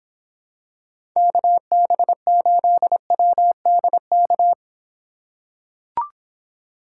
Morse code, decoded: KB8WDK

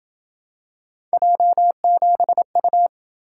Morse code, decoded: J7U

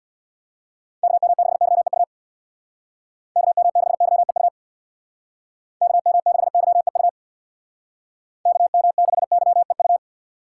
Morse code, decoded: KM6YEU KM6YEU KM6YEU KM6YEU

